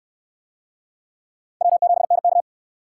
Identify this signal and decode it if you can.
Morse code, heard as a keyed tone, about 68 wpm, 700 Hz. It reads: W7NK